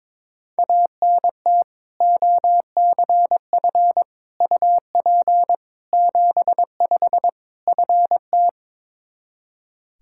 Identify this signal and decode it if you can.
Morse code, heard as a keyed tone, about 22 wpm, 700 Hz. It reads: ANT OCF UP 75 FT
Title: Practice QSO - Exchange 2 - Info - 22wpm